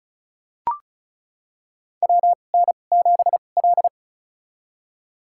Morse code, heard as a keyed tone, about 36 wpm, spaced as for 25 wpm, 700 Hz. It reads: WN7L
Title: USA Call Signs - 4 Characters - 2x Character Spacing - 35wpm